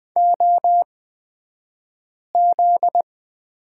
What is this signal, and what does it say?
Morse code, heard as a keyed tone, about 20 wpm, spaced as for 5 wpm, 700 Hz. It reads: OZ
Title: 242 Lesson 39d - Letter z - Two Characters 20wpm